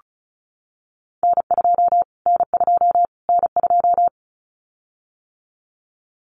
Morse code, heard as a keyed tone, about 34 wpm, 700 Hz. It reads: N2 N2 N2
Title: Callsign Prefixes for Top 25 DXCC - Repeated 3x - 35wpm